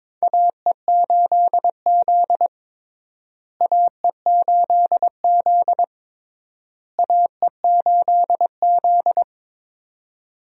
Morse code, heard as a keyed tone, about 22 wpm, 700 Hz. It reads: AE8Z AE8Z AE8Z
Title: USA Call Signs - 4 Characters - Repeated 3x - 22wpm